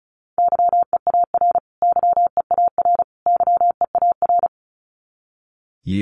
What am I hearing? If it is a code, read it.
Morse code, heard as a keyed tone, about 34 wpm, 700 Hz. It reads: YEAR YEAR YEAR
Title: Top 200 Words Repeated 35wpm